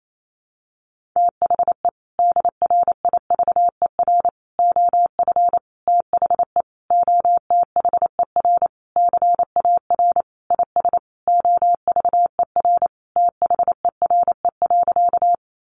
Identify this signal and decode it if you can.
Morse code, heard as a keyed tone, about 28 wpm, 700 Hz. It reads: THE DRIVER OF THE OTHER CAR IS OVER THERE.
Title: Sentences from Top 1000 Words 28wpm